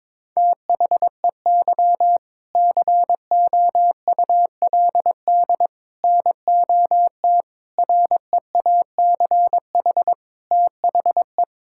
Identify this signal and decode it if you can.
Morse code, heard as a keyed tone, about 22 wpm, 700 Hz. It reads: THEY COULD NOT REACH THE
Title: Sentences from Top 1000 Words - Review Part 6 - 22wpm